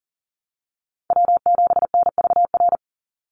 Morse code, heard as a keyed tone, about 40 wpm, 690 Hz. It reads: W7NVR
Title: USA Call Signs 40wpm